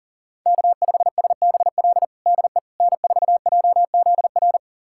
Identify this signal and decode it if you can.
Morse code, heard as a keyed tone, about 40 wpm, 700 Hz. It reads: K5SBL DE N4JZR
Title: USA Callsign DE Callsign - 5 Characters - 40wpm